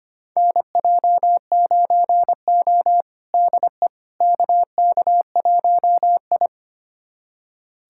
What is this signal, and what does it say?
Morse code, heard as a keyed tone, about 25 wpm, 700 Hz. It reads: NJ9O DE KK1I
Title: USA Callsign DE Callsign - 4 Characters - 25wpm